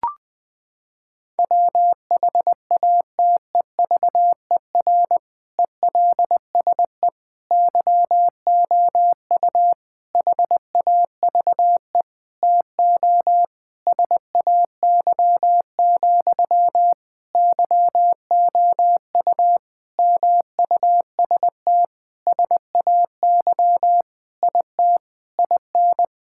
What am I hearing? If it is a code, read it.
Morse code, heard as a keyed tone, about 21 wpm, 700 Hz. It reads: WHATEVER ELSE YOU HAVE TO SAY, YOU MUST SAY IT IN
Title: Sentences from Top 1600 Words 20wpm